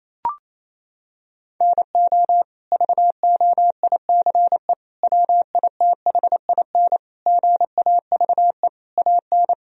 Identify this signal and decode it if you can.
Morse code, heard as a keyed tone, about 28 wpm, 700 Hz. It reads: NO VOICE WITHIN GAVE AN
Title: Sentences from Top 1000 Words - Review Part 4 - 28wpm